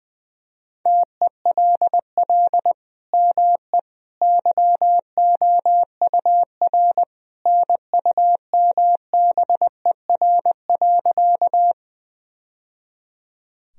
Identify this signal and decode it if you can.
Morse code, heard as a keyed tone, about 21 wpm, 700 Hz. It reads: TELL ME YOUR NUMBER.
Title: Sentences from Top 1000 Words - Review Part 1 - 20wpm